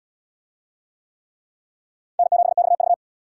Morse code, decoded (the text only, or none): N7XV